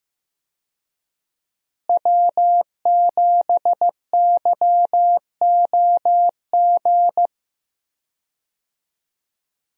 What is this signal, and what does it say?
Morse code, heard as a keyed tone, about 15 wpm, 700 Hz. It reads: W7YOG
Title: USA Call Signs - 5 Characters - Repeated 3x - 15wpm